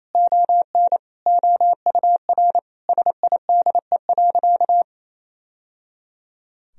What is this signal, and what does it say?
Morse code, heard as a keyed tone, about 28 wpm, 700 Hz. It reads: ON OUR SIDE.
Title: Sentences from Top 1000 Words - Review Part 3 - 28wpm